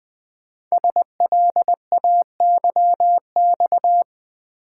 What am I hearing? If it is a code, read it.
Morse code, heard as a keyed tone, about 20 wpm, 700 Hz. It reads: SLAYX
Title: Code Groups - 5 Letters - 20wpm